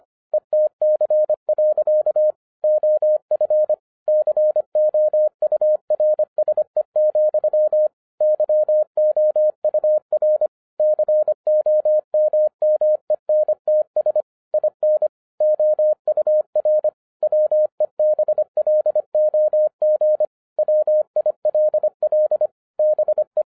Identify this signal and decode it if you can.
Morse code, heard as a keyed tone, about 25 wpm, 610 Hz. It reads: ETC. OF COURSE, YOUR COMMENTS IN OUR WEBLOG WILL BE